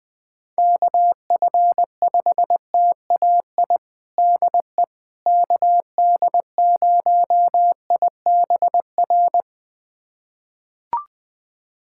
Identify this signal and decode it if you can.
Morse code, heard as a keyed tone, about 20 wpm, 700 Hz. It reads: KF5TAI DE KD0IBR
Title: USA Callsign DE Callsign 20wpm